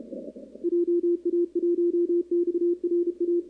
Morse code, heard as a keyed tone, about 29 wpm, 340 Hz, with radio noise. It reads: JA1XRA
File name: contests / 2019-cw-wpx-kt5j-k5pi- -2330 0-11